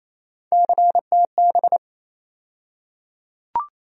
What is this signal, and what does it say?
Morse code, heard as a keyed tone, about 28 wpm, 700 Hz. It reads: CTB
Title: USA Call Sign Postfixes - 3 Letters 28wpm